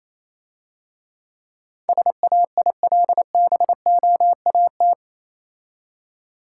Morse code, heard as a keyed tone, about 27 wpm, 700 Hz. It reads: SAILBOAT
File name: Easy Words - Speed-Racing with Hints - 22wpm